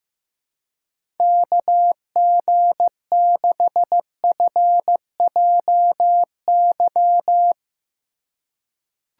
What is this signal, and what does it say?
Morse code, heard as a keyed tone, about 15 wpm, 700 Hz. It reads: KG6FJY
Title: USA Call Signs - 6 Characters - 3x Character Spacing - 15wpm